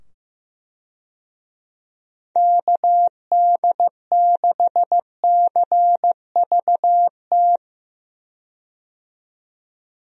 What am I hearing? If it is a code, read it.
Morse code, heard as a keyed tone, about 15 wpm, 700 Hz. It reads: KD6CVT
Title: USA Call Signs - 6 Characters - 15wpm